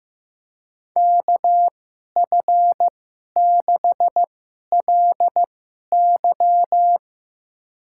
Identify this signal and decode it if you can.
Morse code, heard as a keyed tone, about 15 wpm, spaced as for 11 wpm, 700 Hz. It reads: KF6LY
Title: USA Call Signs - 5 Characters - 2x Character Spacing - 15wpm